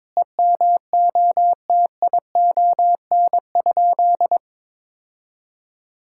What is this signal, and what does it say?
Morse code, heard as a keyed tone, about 22 wpm, 700 Hz. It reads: EMOTION?
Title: Sentences from Top 1600 Words 22wpm